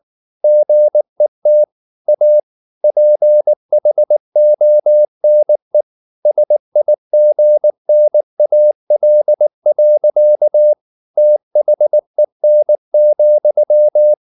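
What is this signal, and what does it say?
Morse code, heard as a keyed tone, about 19 wpm, 600 Hz. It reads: GET A PHONE SIGNAL. THEN,
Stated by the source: Morse code at 19 wpm from CW med Gnister, reading Ken Follett's novel Never in English.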